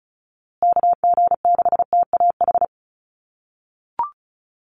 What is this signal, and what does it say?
Morse code, heard as a keyed tone, about 34 wpm, 700 Hz. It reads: KG6TAH
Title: USA Call Signs Repeated 3x 35wpm